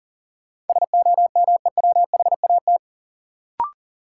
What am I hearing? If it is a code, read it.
Morse code, heard as a keyed tone, about 40 wpm, 700 Hz. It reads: SOMEWHAT
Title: Sentences from Top 1000 Words - Review Part 1 - New Word Drill - 40wpm